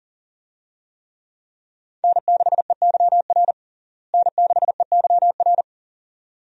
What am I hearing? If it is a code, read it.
Morse code, heard as a keyed tone, about 37 wpm, 700 Hz. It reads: N6EYR N6EYR
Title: USA Call Signs - 5 Characters - Repeated 2x - 40wpm